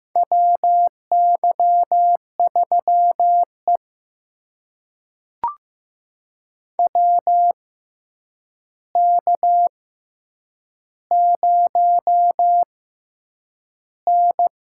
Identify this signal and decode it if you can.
Morse code, heard as a keyed tone, about 15 wpm, spaced as for 8 wpm, 700 Hz. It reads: WY3E WK0N
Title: USA Call Signs - 4 Characters - 6x Character Spacing - 15wpm